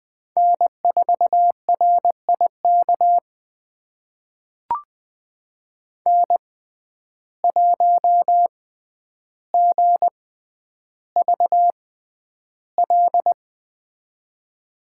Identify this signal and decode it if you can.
Morse code, heard as a keyed tone, about 21 wpm, spaced as for 10 wpm, 700 Hz. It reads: N4RIK N1GVL
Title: USA Call Signs - 5 Characters - 6x Character Spacing - 20wpm